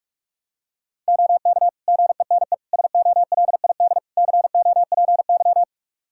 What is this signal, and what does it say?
Morse code, heard as a keyed tone, about 41 wpm, 700 Hz. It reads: OK GENE SOLID COPY
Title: Practice QSO - Exchange 3 - More Info - 45wpm